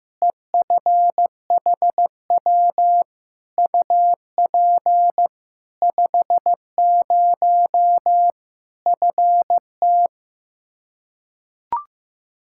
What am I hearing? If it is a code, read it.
Morse code, heard as a keyed tone, about 15 wpm, 700 Hz. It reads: EFHW UP 50 FT
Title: Practice QSO - Exchange 2 - Info - 15wpm